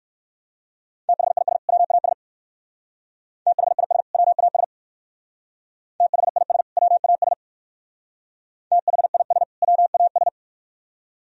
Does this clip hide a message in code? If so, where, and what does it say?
Yes: Morse code from 1.1 to 10.3 s, heard as a keyed tone, about 58 wpm, 700 Hz. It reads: THIS WAS THIS WAS THIS WAS THIS WAS